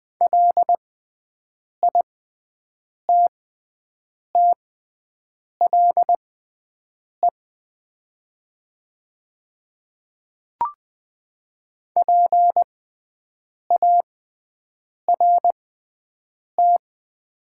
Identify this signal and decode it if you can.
Morse code, heard as a keyed tone, about 20 wpm, spaced as for 7 wpm, 700 Hz. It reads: LITTLE PART
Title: 6x Character Spacing - Top 200 Words - 20wpm